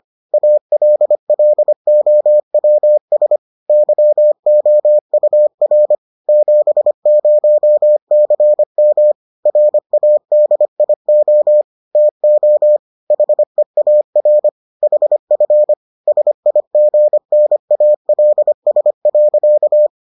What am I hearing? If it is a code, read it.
Morse code, heard as a keyed tone, about 25 wpm, 600 Hz. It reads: ALLOWS YOUR 70CM RADIO TO HEAR HF SIGNALS.